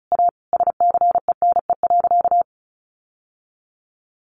Morse code, heard as a keyed tone, about 34 wpm, 700 Hz. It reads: A SCENE.
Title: Sentences from Top 1100 Words 35wpm